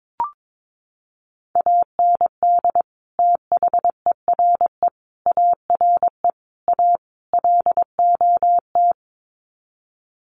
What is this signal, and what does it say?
Morse code, heard as a keyed tone, about 22 wpm, 700 Hz. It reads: AND THERE ARE A LOT
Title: Sets of 5 Words Encore2 22wpm